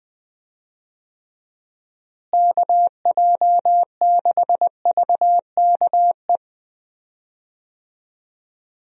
Morse code, heard as a keyed tone, about 20 wpm, 700 Hz. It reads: KJ6VKE
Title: USA Call Signs - 6 Characters - Repeated 3x - 20wpm